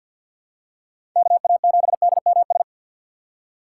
Morse code, heard as a keyed tone, about 52 wpm, 700 Hz. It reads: KA7DMS
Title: USA Call Signs - 6 Characters - 3x Character Spacing - 50wpm